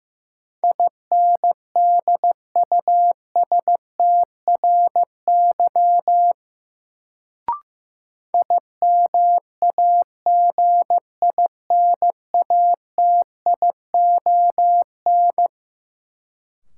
Morse code, heard as a keyed tone, about 15 wpm, 700 Hz. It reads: INDUSTRY IMAGINATION
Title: Sentences from Top 1300 Words - New Word Drill Alt - 15wpm